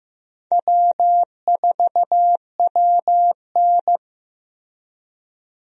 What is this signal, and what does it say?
Morse code, heard as a keyed tone, about 15 wpm, 700 Hz. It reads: W4WN
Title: USA Call Signs - 4 Characters - Repeated 2x - 15wpm